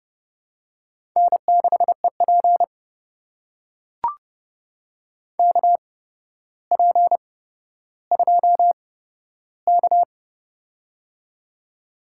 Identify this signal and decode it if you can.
Morse code, heard as a keyed tone, about 30 wpm, spaced as for 13 wpm, 700 Hz. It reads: N6EP KP2K
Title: USA Call Signs - 4 Characters - 8x Character Spacing - 30wpm